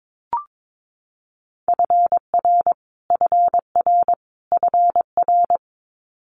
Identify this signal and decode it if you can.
Morse code, heard as a keyed tone, about 22 wpm, 700 Hz. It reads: FR FR FR
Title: Multi-letter Phonemes Repeated 3x 22wpm